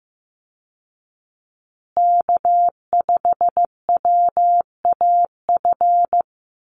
Morse code, heard as a keyed tone, about 15 wpm, 700 Hz. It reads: K5WAF
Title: Call Signs Repeated 2x - 15wpm